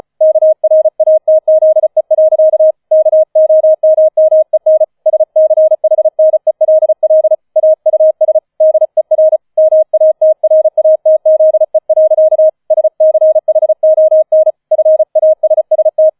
Morse code, heard as a keyed tone, about 35 wpm, 630 Hz. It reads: KRATZE. KOMMEN SCHNELL AUS DER MATRATZE. SCHON FASST
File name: 005 Fünfter Streich 35wpm